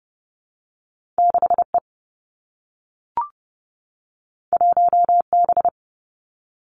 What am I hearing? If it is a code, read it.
Morse code, heard as a keyed tone, about 31 wpm, 700 Hz. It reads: 6E 1B
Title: Two Challenging Letter-Number Combinations 30wpm